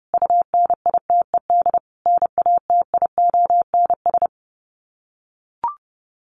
Morse code, heard as a keyed tone, about 30 wpm, 700 Hz. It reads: UNITED NATIONS
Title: Sets of 5 Words Encore2 30wpm